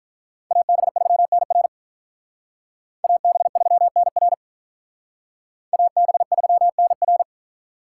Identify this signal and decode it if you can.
Morse code, heard as a keyed tone, about 44 wpm, 700 Hz. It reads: AB3NR AB3NR AB3NR